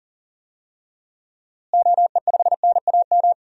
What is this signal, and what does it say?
Morse code, heard as a keyed tone, about 37 wpm, 700 Hz. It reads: OE5NAM
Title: DXCC-100 Call Signs - 6 Characters - Repeated 2x - 40wpm